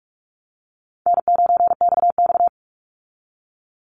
Morse code, heard as a keyed tone, about 44 wpm, 700 Hz. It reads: N9XX
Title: Call Signs Repeated 2x - 45wpm